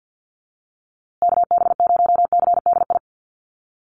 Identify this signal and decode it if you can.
Morse code, heard as a keyed tone, about 50 wpm, 700 Hz. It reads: KB0CDI
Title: USA Call Signs Repeated 2x 50wpm